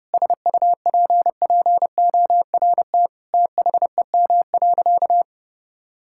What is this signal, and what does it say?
Morse code, heard as a keyed tone, about 32 wpm, 700 Hz. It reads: SUPPORT THEM.
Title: Sentences from Top 1000 Words - Review Part 5 - 30wpm